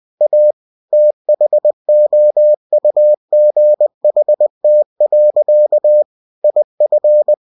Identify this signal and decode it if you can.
Morse code, heard as a keyed tone, about 20 wpm, 600 Hz. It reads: A THOUGHT. IF
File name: cw-books / winnie-the-pooh / winnie-20wpm-page-0063